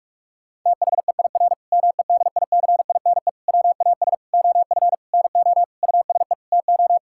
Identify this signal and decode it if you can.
Morse code, heard as a keyed tone, about 46 wpm, 700 Hz. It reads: THEIR MEDICINE WAS OF NO USE TO
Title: Sentences from Top 1000 Words - Review Part 6 - 45wpm